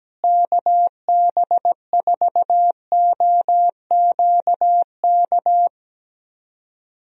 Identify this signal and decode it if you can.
Morse code, heard as a keyed tone, about 17 wpm, 700 Hz. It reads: KB4OQK
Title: USA Call Signs - 6 Characters - Repeated 3x - 17wpm